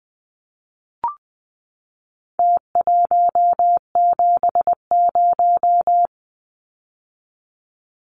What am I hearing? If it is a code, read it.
Morse code, heard as a keyed tone, about 20 wpm, 700 Hz. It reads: T170
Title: Numbers - Contest Style - 4-Digit 20wpm